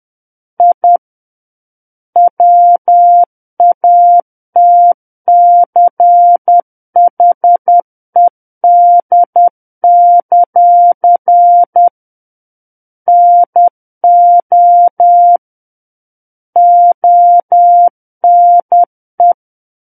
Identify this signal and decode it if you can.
Morse code, heard as a keyed tone, about 10 wpm, 700 Hz. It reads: I WATCHED; NO ONE
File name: Book One / 01-Chapter 1 - 10 WPM